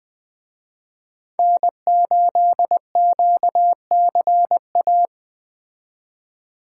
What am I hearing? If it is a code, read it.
Morse code, heard as a keyed tone, about 20 wpm, 700 Hz. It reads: N8QCA
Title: Speed Racing - USA 1x3 Call Signs 17wpm